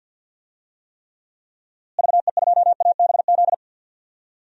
Morse code, heard as a keyed tone, about 53 wpm, 700 Hz. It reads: VE2ABZ
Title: DXCC-100 Call Signs - 6 Characters - Repeated 2x - 50wpm